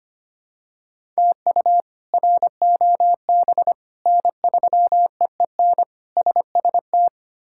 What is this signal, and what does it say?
Morse code, heard as a keyed tone, about 25 wpm, 700 Hz. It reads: TU ROB N3EEN SST
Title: K1USN SST QSO Practice - 25wpm